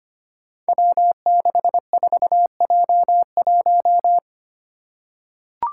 Morse code, heard as a keyed tone, about 25 wpm, 700 Hz. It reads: W64J1
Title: Five Challenging Letter-Number Combinations 25wpm